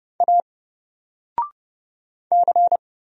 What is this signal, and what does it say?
Morse code, heard as a keyed tone, about 31 wpm, 700 Hz. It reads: A C
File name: Single-Letter-Number-30wpm